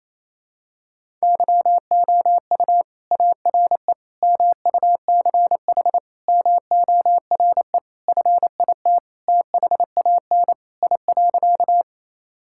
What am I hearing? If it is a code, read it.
Morse code, heard as a keyed tone, about 28 wpm, 700 Hz. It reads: YOU ARE MUCH MORE FIT THAN I.
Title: Sentences from Top 1000 Words - Review Part 5 - 28wpm